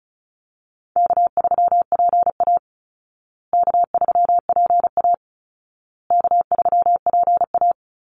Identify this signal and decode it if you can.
Morse code, heard as a keyed tone, about 34 wpm, 700 Hz. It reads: K3PA K3PA K3PA
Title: Easy Call Signs Repeated 3x - 35wpm